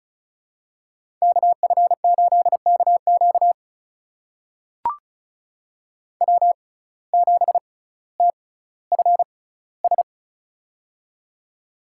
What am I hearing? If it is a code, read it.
Morse code, heard as a keyed tone, about 36 wpm, spaced as for 18 wpm, 700 Hz. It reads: KF8KQ W7TFS
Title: USA Call Signs - 5 Characters - 6x Character Spacing - 35wpm